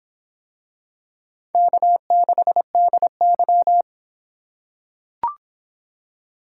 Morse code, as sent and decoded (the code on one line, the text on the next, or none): -.- -.... -.. -.--
K6DY